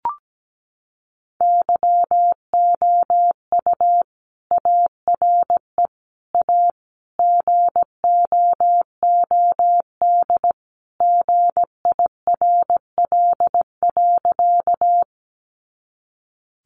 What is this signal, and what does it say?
Morse code, heard as a keyed tone, about 17 wpm, 700 Hz. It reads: YOU ARE A GOOD GIRL.
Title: Sentences from Top 400 Words 17wpm